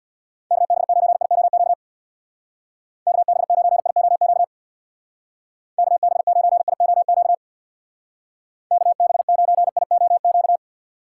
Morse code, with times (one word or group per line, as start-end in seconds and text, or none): 0.5-1.7 KB9IOQ
3.1-4.4 KB9IOQ
5.8-7.3 KB9IOQ
8.7-10.6 KB9IOQ